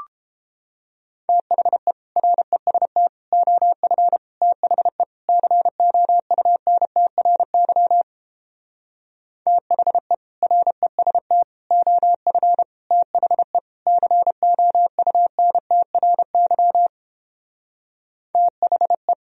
Morse code, heard as a keyed tone, about 29 wpm, 700 Hz. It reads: THE REST OF THE COUNTRY THE REST OF THE COUNTRY THE